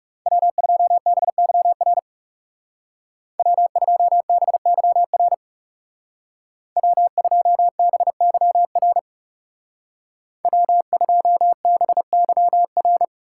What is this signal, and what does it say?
Morse code, heard as a keyed tone, about 36 wpm, 700 Hz. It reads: W2BYR W2BYR W2BYR W2BYR
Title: Speed Racing - USA 1x3 Call Signs 30wpm